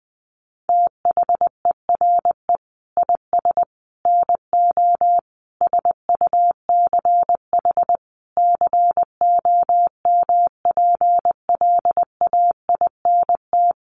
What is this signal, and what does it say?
Morse code, heard as a keyed tone, about 20 wpm, 700 Hz. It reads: THERE IS NO SUCH COMPLAINT
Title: Sentences from Top 1400 Words 20wpm